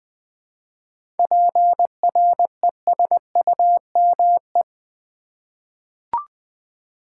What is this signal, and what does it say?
Morse code, heard as a keyed tone, about 20 wpm, 700 Hz. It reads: PRESUME
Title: 7 Letter Words - First 3 Letters Given - 20wpm